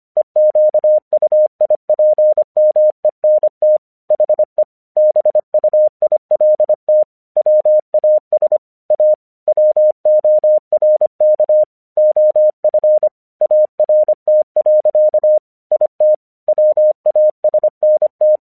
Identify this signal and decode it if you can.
Morse code, heard as a keyed tone, about 25 wpm, 610 Hz. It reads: EQUIPMENT HE BUILT WAS A WORK OF ART. IT WASNT
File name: 1 English 25wpm / AQC11